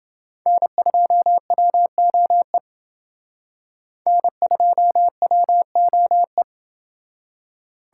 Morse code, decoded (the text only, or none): N2WOE N2WOE